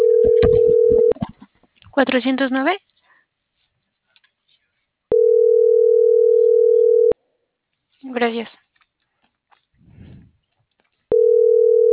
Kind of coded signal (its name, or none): telephone call progress tone